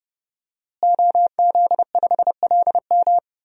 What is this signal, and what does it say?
Morse code, heard as a keyed tone, about 30 wpm, 700 Hz. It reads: OZ5LM
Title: DXCC-100 Call Signs - 5 Characters - Repeated 3x - 30wpm